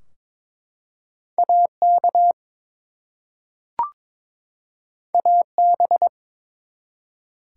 Morse code, heard as a keyed tone, about 22 wpm, 700 Hz. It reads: AK AB
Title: US States and Canadian Provinces 22wpm